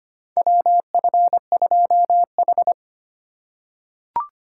Morse code, heard as a keyed tone, about 25 wpm, 700 Hz. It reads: WF2H
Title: USA Call Signs - 4 Characters - 4x Character Spacing - 25wpm